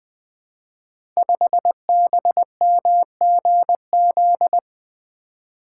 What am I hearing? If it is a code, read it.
Morse code, heard as a keyed tone, about 20 wpm, 700 Hz. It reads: 5BMGZ
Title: Code Groups - 5 Letter:Numbers - 20wpm